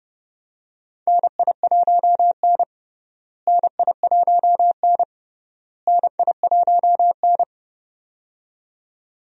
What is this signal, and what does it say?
Morse code, heard as a keyed tone, about 30 wpm, 700 Hz. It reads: NI1N NI1N NI1N
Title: USA Call Signs - 4 Characters - Repeated 3x - 30wpm